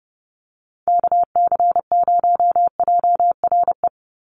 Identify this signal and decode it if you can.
Morse code, heard as a keyed tone, about 30 wpm, 700 Hz. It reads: KC0JRE
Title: USA Call Signs 30wpm